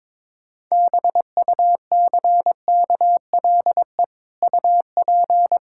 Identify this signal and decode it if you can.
Morse code, heard as a keyed tone, about 22 wpm, 700 Hz. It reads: BUCKLE UP
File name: practice / 22wpm / Easy Sentences with Hints - 22wpm